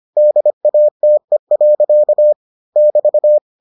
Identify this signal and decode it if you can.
Morse code, heard as a keyed tone, about 25 wpm, 600 Hz. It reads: DATE. <BT>